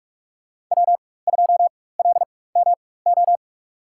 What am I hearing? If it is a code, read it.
Morse code, heard as a keyed tone, about 46 wpm, spaced as for 21 wpm, 700 Hz. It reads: W2LMO
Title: USA Call Signs - 5 Characters - 4x Character Spacing - 45wpm